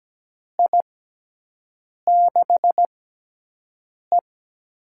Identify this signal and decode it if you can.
Morse code, heard as a keyed tone, about 17 wpm, spaced as for 6 wpm, 700 Hz. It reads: I6E
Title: USA Call Signs - 4 Characters - 6x Character Spacing - 17wpm